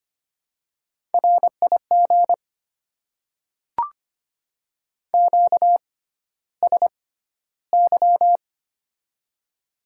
Morse code, heard as a keyed tone, about 25 wpm, spaced as for 13 wpm, 700 Hz. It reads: RIG QSY
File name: Top 90 QSO Elements - 6x Character Spacing - 25wpm